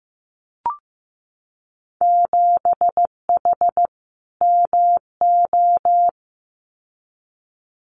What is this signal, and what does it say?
Morse code, heard as a keyed tone, about 15 wpm, 700 Hz. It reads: 7H MO
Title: Winter Field Day Exchanges Encore 15wpm